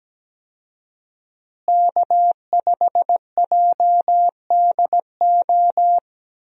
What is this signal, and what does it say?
Morse code, heard as a keyed tone, about 17 wpm, 700 Hz. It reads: K5JDO
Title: USA Call Signs - 5 Characters - Repeated 3x - 17wpm